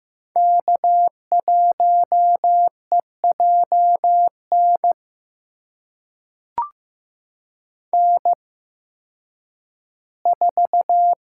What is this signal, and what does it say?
Morse code, heard as a keyed tone, about 15 wpm, spaced as for 10 wpm, 700 Hz. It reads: K1EJN N4
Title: USA Call Signs - 5 Characters - 8x Character Spacing - 15wpm